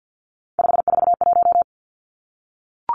Morse code, heard as a keyed tone, about 50 wpm, 700 Hz. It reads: 541
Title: Three Challenging Letter-Number Combinations 50wpm